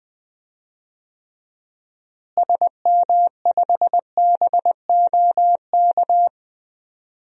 Morse code, heard as a keyed tone, about 21 wpm, 700 Hz. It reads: SM5BOK
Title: DXCC-100 Call Signs - 6 Characters - Repeated 3x - 20wpm